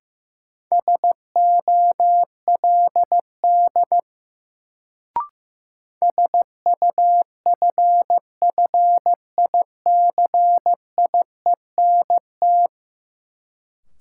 Morse code, heard as a keyed tone, about 15 wpm, 700 Hz. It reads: SOLD SUFFICIENT
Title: Sentences from Top 1400 Words - New Word Drill - 15wpm